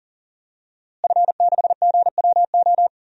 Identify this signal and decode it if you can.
Morse code, heard as a keyed tone, about 40 wpm, 700 Hz. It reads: F6GWO
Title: DXCC-100 Call Signs - 5 Characters - Repeated 3x - 40wpm